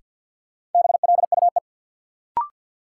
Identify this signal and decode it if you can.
Morse code, heard as a keyed tone, about 53 wpm, 700 Hz. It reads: BLUE
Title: Sentences from Top 1200 Words - New Word Drill Alt - 50wpm